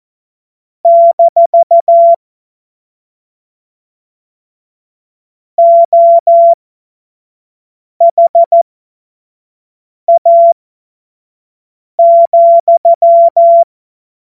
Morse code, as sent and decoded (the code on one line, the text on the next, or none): -....- / --- .... .- --..--
- OHA,